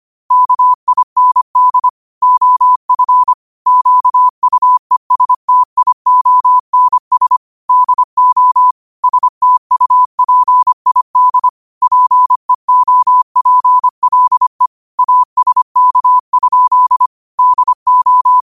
Morse code, heard as a keyed tone, about 25 wpm, 1000 Hz. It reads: KIND OF QUESTIONS DO STUPID PEOPLE ASK? DO
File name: qotdfiles / 25wpm